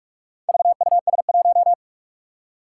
Morse code, heard as a keyed tone, about 44 wpm, 700 Hz. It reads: VUS1
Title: Four Challenging Letter-Number Combinations 45wpm